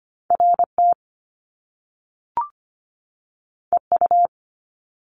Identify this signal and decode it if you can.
Morse code, heard as a keyed tone, about 25 wpm, 700 Hz. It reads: RT EU
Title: Top Two Letter Combinations in Words 25wpm